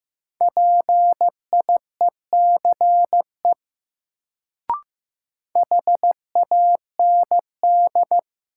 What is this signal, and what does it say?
Morse code, heard as a keyed tone, about 15 wpm, 700 Hz. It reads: PIECE HAND
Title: Sets of 3 Words - New Word Drill - 15wpm